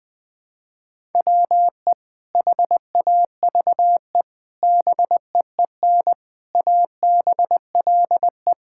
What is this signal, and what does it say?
Morse code, heard as a keyed tone, about 20 wpm, 700 Hz. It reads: WE HAVE BEEN ABLE
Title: Sentences with 4-Word Phrases - 20wpm